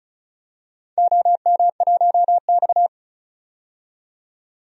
Morse code, heard as a keyed tone, about 36 wpm, 700 Hz. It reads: OM1X
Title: DXCC-100 Call Signs - 4 Characters - 35wpm